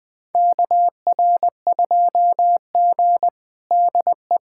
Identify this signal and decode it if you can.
Morse code, heard as a keyed tone, about 21 wpm, 700 Hz. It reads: KR2G DE
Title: USA Callsign DE Callsign - 4 Characters - 20wpm